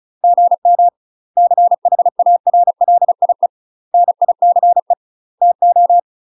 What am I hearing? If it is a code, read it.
Morse code, heard as a keyed tone, about 35 wpm, 700 Hz. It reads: GM CHARLIE NICE TO
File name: UK MASTER QSOs TRAINING / 35WPM / UK MASTER QSOs 0026